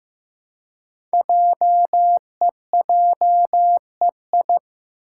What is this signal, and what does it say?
Morse code, heard as a keyed tone, about 15 wpm, 700 Hz. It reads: JEJEI
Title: Code Groups - 5 Letter:Numbers - 15wpm